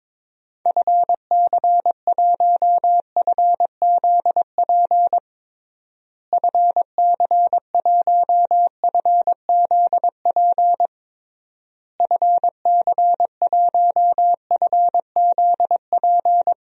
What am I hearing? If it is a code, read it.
Morse code, heard as a keyed tone, about 22 wpm, 700 Hz. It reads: FC1FZP FC1FZP FC1FZP
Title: DXCC-100 Call Signs - 6 Characters - Repeated 3x - 22wpm